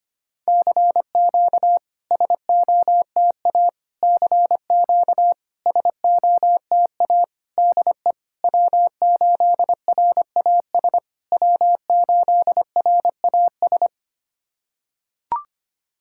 Morse code, heard as a keyed tone, about 25 wpm, 700 Hz. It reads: CQ SOTA CQ SOTA DE W8RAS W8RAS
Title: SOTA QSO Practice 25wpm